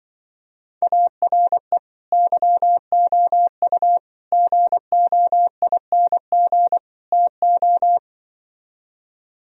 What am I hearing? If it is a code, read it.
Morse code, heard as a keyed tone, about 25 wpm, 700 Hz. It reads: ARE YOU GOING TO